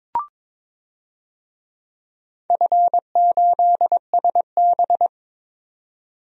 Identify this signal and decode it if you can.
Morse code, heard as a keyed tone, about 22 wpm, 700 Hz. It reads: F8SB
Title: DXCC-100 Call Signs - 4 Characters - 22wpm